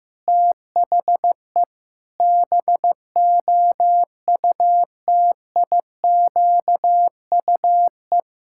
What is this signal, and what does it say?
Morse code, heard as a keyed tone, about 15 wpm, 700 Hz. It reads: THE BOUTIQUE
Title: Pangram Challenge v2 - 15wpm